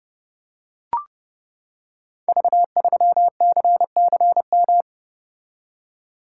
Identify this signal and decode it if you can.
Morse code, heard as a keyed tone, about 30 wpm, 700 Hz. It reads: V3CCM
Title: Code Groups - 5 Letter:Numbers - 30wpm